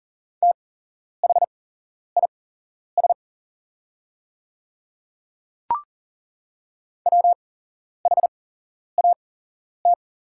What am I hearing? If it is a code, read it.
Morse code, heard as a keyed tone, about 41 wpm, spaced as for 11 wpm, 700 Hz. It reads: THIS WHAT
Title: 8x Character Spacing - Top 100 Words - 40wpm